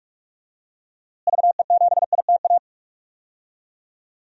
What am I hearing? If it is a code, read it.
Morse code, heard as a keyed tone, about 46 wpm, 700 Hz. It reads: VE7ITA
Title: DXCC-100 Call Signs - 6 Characters - Repeated 2x - 45wpm